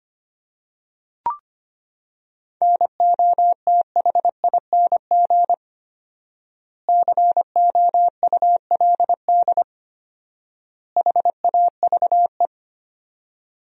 Morse code, heard as a keyed tone, about 25 wpm, 700 Hz. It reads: NOTHING COULD HAVE
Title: 4x Word-Spacing - Sentences from Top 500 Words - 25wpm